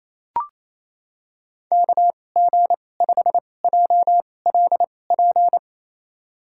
Morse code, heard as a keyed tone, about 28 wpm, spaced as for 20 wpm, 690 Hz. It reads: KG5JLP
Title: USA Call Signs - 6 Characters - 2x Character Spacing - 28wpm